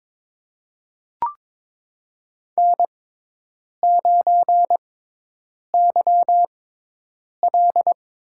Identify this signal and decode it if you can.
Morse code, heard as a keyed tone, about 21 wpm, spaced as for 8 wpm, 700 Hz. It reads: N9YL
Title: USA Call Signs - 5 Characters - 6x Character Spacing - 22wpm